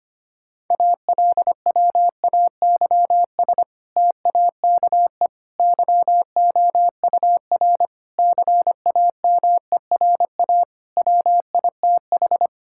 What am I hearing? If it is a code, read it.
Morse code, heard as a keyed tone, about 25 wpm, 700 Hz. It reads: ALWAYS TAKE YOUR CAMERA WITH
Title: Sentences from Top 1000 Words - Review Part 5 - 25wpm